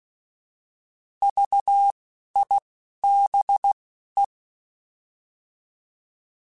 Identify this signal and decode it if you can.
Morse code, heard as a keyed tone, about 16 wpm, spaced as for 11 wpm, 800 Hz. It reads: VIBE
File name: Alphabetical / Alphabetical mp3s / Vs @10